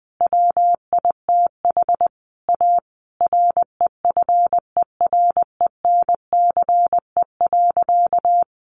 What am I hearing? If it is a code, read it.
Morse code, heard as a keyed tone, about 20 wpm, 700 Hz. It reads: WITH A REFERENCE.
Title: Sentences from Top 1200 Words 20wpm